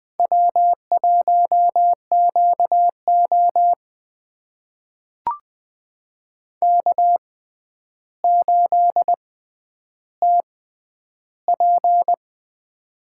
Morse code, heard as a keyed tone, about 21 wpm, spaced as for 10 wpm, 700 Hz. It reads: W1QO K8TP
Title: USA Call Signs - 4 Characters - 6x Character Spacing - 20wpm